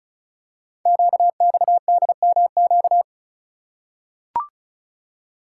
Morse code, heard as a keyed tone, about 36 wpm, 700 Hz. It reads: QXDMQ
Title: Code Groups - 5 Letters - 35wpm